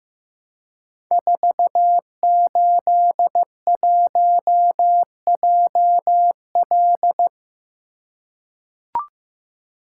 Morse code, heard as a keyed tone, about 15 wpm, 700 Hz. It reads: 481JL
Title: Code Groups - 5 Letter:Numbers - 15wpm